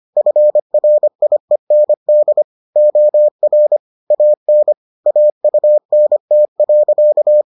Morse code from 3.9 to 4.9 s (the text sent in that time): AN